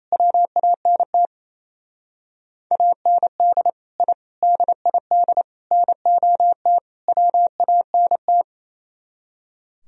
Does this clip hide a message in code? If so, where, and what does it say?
Yes: Morse code from 0.1 to 8.4 s, heard as a keyed tone, about 35 wpm, 700 Hz. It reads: WANT AND I DID NOT WANT